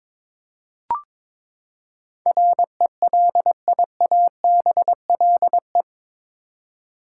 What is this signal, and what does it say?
Morse code, heard as a keyed tone, about 22 wpm, 700 Hz. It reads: RELIABLE
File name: practice / 22wpm / Eight Letter Words - 22wpm